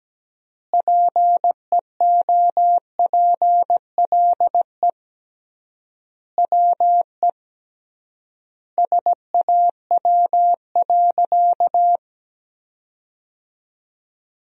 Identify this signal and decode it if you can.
Morse code, heard as a keyed tone, about 17 wpm, 700 Hz. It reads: PEOPLE WE SAW.
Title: 3x Word-Spacing - Sentences from Top 300 Words - 17wpm